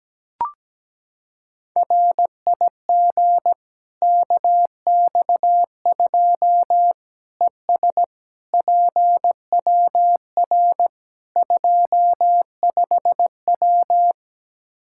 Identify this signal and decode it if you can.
Morse code, heard as a keyed tone, about 17 wpm, 700 Hz. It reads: RIG KX2 ES PWR 25W
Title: Practice QSO - Exchange 2 - Info - 17wpm